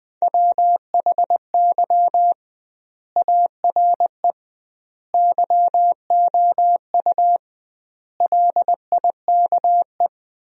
Morse code, heard as a keyed tone, about 21 wpm, 700 Hz. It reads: WHY ARE YOU LIKE
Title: 2x Word-Spacing - Sentences from Top 200 Words - 20wpm